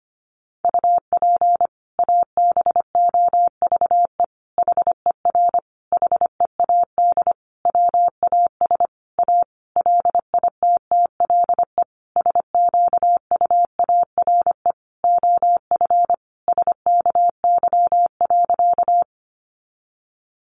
Morse code, heard as a keyed tone, about 25 wpm, 700 Hz. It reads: UP ABOVE HER HEAD WAS A LITTLE SQUARE OF SKY.
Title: Sentences from Top 1200 Words 25wpm